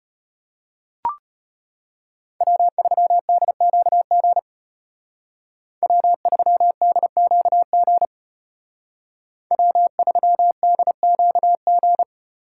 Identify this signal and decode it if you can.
Morse code, heard as a keyed tone, about 34 wpm, 700 Hz. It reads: W3DQG W3DQG W3DQG